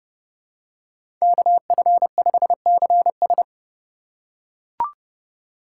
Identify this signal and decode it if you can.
Morse code, heard as a keyed tone, about 30 wpm, 700 Hz. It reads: KF5CS
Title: USA Call Signs - 5 Characters - 2x Character Spacing - 30wpm